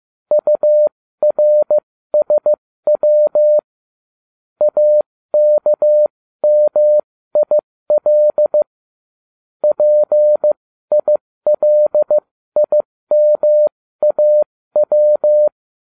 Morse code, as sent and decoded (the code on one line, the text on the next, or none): ..- .-. ... .-- / .- -.- -- .. .-.. / .--. .. .-.. .. -- .- .--
URSW AKMIL PILIMAW